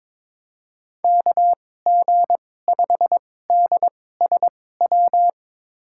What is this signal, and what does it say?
Morse code, heard as a keyed tone, about 22 wpm, spaced as for 16 wpm, 700 Hz. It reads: KG5DSW
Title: USA Call Signs - 6 Characters - 2x Character Spacing - 22wpm